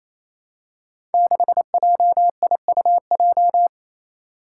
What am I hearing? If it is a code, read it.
Morse code, heard as a keyed tone, about 28 wpm, 700 Hz. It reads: 6JIUJ
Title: Five Challenging Letter-Number Combinations 28wpm